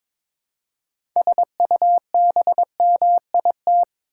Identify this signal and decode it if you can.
Morse code, heard as a keyed tone, about 21 wpm, 700 Hz. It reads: SUBMIT
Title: Six Letter Words - 22wpm